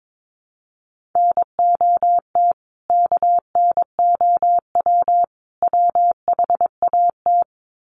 Morse code, heard as a keyed tone, about 22 wpm, 700 Hz. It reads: NOT KNOW WHAT
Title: Sets of 3 Words 22wpm